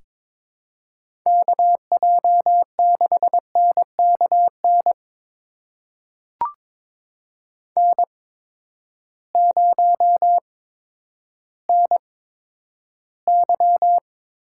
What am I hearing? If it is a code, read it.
Morse code, heard as a keyed tone, about 22 wpm, spaced as for 11 wpm, 700 Hz. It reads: KJ6NKN N0NY
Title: USA Call Signs - 8x Character Spacing - 22wpm